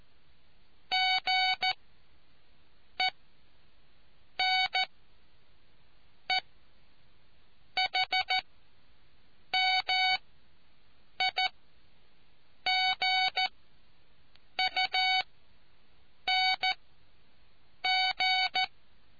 Morse code, heard as a keyed tone, about 14 wpm, spaced as for 6 wpm, 750 Hz. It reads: GENEHMIGUNG